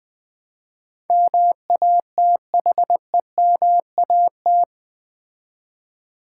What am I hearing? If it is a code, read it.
Morse code, heard as a keyed tone, about 20 wpm, 700 Hz. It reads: MATHEMAT
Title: Build A Word - 4 Letters At A Time - 12-Letter Words - 20wpm